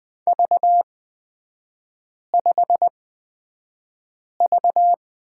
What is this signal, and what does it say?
Morse code, heard as a keyed tone, about 20 wpm, spaced as for 5 wpm, 700 Hz. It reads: V5V
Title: 255 Lesson 41e - Period - Three Characters 20wpm